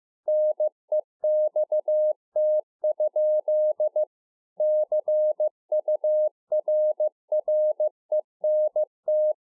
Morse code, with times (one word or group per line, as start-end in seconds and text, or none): 0.3-4.0 NEXT?
4.6-9.3 CURRENT